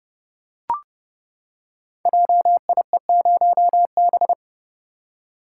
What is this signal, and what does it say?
Morse code, heard as a keyed tone, about 30 wpm, 700 Hz. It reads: JIE0B
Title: Code Groups - 5 Letter:Numbers - 30wpm